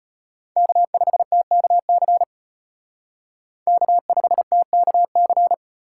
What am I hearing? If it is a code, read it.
Morse code, heard as a keyed tone, about 32 wpm, 700 Hz. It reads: K5TKC K5TKC